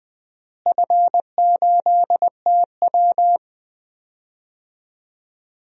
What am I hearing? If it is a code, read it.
Morse code, heard as a keyed tone, about 21 wpm, 700 Hz. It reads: F8TW
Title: DXCC-100 Call Signs - 4 Characters - Repeated 2x - 20wpm